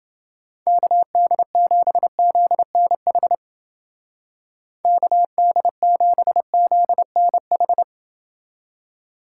Morse code, heard as a keyed tone, about 29 wpm, 700 Hz. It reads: KD7ZNH KD7ZNH